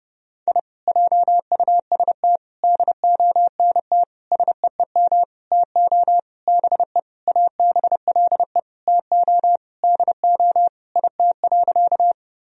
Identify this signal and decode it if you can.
Morse code, heard as a keyed tone, about 29 wpm, 700 Hz. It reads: I JUST DONT SEEM TO BE ABLE TO DO IT.
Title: Sentences from Top 1000 Words - Review Part 2 - 30wpm